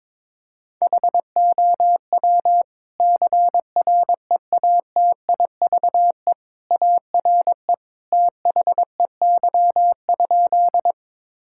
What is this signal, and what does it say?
Morse code, heard as a keyed tone, about 22 wpm, 700 Hz. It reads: HOW CREATIVE ARE THEY?
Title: Sentences from Top 1000 Words - Review Part 5 - 22wpm